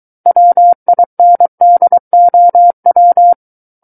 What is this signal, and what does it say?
Morse code, heard as a keyed tone, about 23 wpm, 700 Hz. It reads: WINDOW
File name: Book One / 14-Chapter 14 - 23 WPM